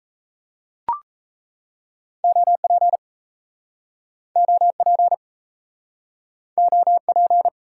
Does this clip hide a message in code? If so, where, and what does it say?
Yes: Morse code from 2.2 to 7.5 s, heard as a keyed tone, about 37 wpm, 700 Hz. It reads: OP OP OP